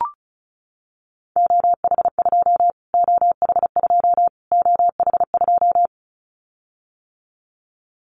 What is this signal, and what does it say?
Morse code, heard as a keyed tone, about 34 wpm, 700 Hz. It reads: OH2 OH2 OH2
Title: Callsign Prefixes for Top 25 DXCC - Repeated 3x - 35wpm